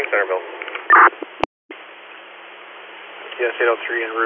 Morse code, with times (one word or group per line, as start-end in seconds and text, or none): none